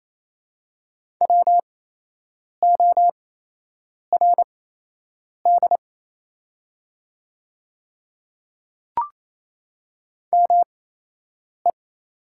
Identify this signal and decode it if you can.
Morse code, heard as a keyed tone, about 28 wpm, spaced as for 8 wpm, 690 Hz. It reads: WORD ME
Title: 8x Character Spacing - Top 200 Words - 28wpm